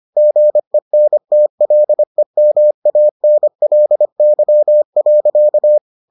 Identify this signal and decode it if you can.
Morse code, heard as a keyed tone, about 25 wpm, 600 Hz. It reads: GENTLEMANLY.